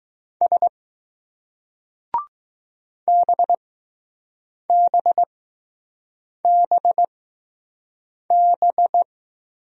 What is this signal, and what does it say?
Morse code, heard as a keyed tone, about 20 wpm, spaced as for 6 wpm, 700 Hz. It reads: S BBBB